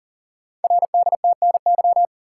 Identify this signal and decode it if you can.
Morse code, heard as a keyed tone, about 41 wpm, 690 Hz. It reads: RDTNY
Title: Code Groups - 5 Letters - 40wpm